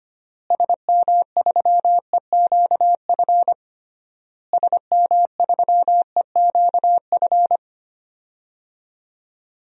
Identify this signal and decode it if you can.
Morse code, heard as a keyed tone, about 25 wpm, 700 Hz. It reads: SM3EQF SM3EQF
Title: DXCC-100 Call Signs - 6 Characters - Repeated 3x - 25wpm